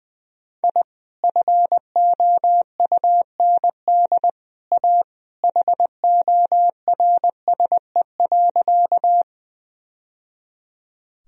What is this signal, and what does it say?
Morse code, heard as a keyed tone, about 21 wpm, 700 Hz. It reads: I FOUND A HORSE.
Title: Sentences from Top 1000 Words - Review Part 1 - 20wpm